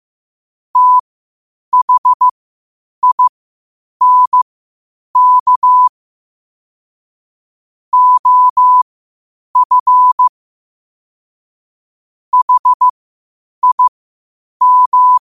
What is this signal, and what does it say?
Morse code, heard as a keyed tone, about 15 wpm, spaced as for 8 wpm, 1000 Hz. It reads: THINK OF HIM
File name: qotdfiles / 8wpm